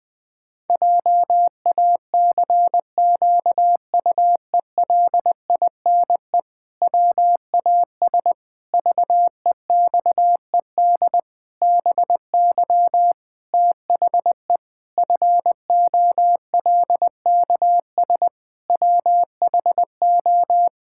Morse code, heard as a keyed tone, about 20 wpm, 700 Hz. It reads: JACQUELINE WAS VEXED BY THE FOLKS WHO
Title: Pangram Challenge v2 - 20wpm